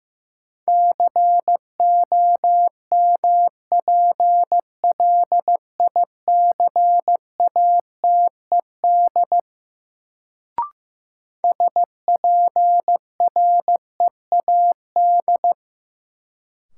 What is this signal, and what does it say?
Morse code, heard as a keyed tone, about 15 wpm, 700 Hz. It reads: COMPLICATED SPREAD
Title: Sentences from Top 1200 Words - New Word Drill Alt - 15wpm